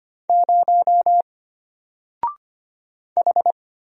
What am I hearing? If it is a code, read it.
Morse code, heard as a keyed tone, about 25 wpm, 700 Hz. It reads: 0 H
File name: Single-Letter-Number-25wpm